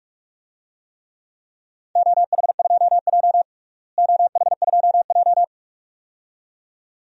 Morse code, heard as a keyed tone, about 46 wpm, 700 Hz. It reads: OH2J OH2J
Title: DXCC-100 Call Signs - 4 Characters - Repeated 2x - 45wpm